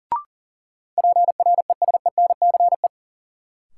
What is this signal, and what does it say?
Morse code, heard as a keyed tone, about 37 wpm, 700 Hz. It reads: PRESENCE
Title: Sets of 3 Words - New Word Drill - 40wpm